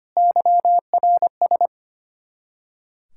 Morse code, heard as a keyed tone, about 33 wpm, 700 Hz. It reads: YRS